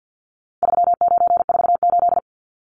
Morse code, heard as a keyed tone, about 50 wpm, 700 Hz. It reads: 3948
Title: Numbers - 4-Digit 50wpm